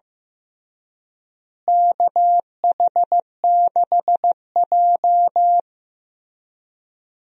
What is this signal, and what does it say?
Morse code, heard as a keyed tone, about 15 wpm, 700 Hz. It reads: KH6J
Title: USA Call Signs - 4 Characters - Repeated 3x - 15wpm